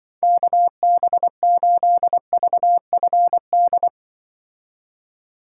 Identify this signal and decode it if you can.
Morse code, heard as a keyed tone, about 24 wpm, 700 Hz. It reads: KB8VFD
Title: Speed Racing - USA 2x3 Call Signs 20wpm